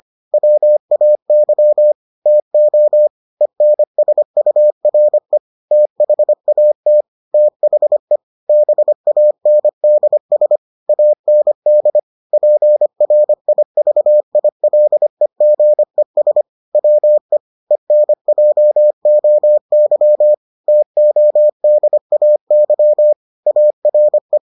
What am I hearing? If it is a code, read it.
Morse code, heard as a keyed tone, about 25 wpm, 600 Hz. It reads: WAY TO ENSURE THAT THE BANDS AND PRIVILEGES WE ENJOY TODAY ARE